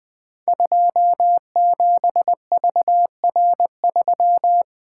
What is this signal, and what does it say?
Morse code, heard as a keyed tone, about 20 wpm, 700 Hz. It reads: 27VR3
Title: Code Groups - 5 Letter:Numbers - 20wpm